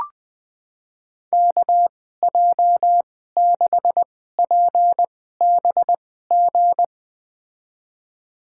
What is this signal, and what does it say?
Morse code, heard as a keyed tone, about 20 wpm, spaced as for 15 wpm, 700 Hz. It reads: KJ6PBG
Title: USA Call Signs - 2x Character Spacing - 20wpm